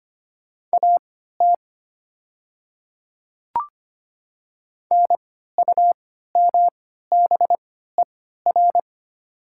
Morse code, heard as a keyed tone, about 25 wpm, spaced as for 14 wpm, 700 Hz. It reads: AT NUMBER
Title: 3x Character Spacing - Top 100 Words - 25wpm